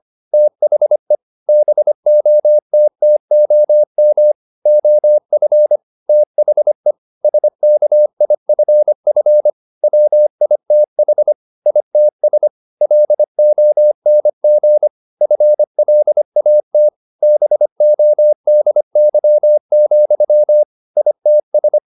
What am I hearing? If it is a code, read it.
Morse code, heard as a keyed tone, about 25 wpm, 600 Hz. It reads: THE BOTTOM OF THE SKIFF WITH ITS LONG FLAT BODY, ITS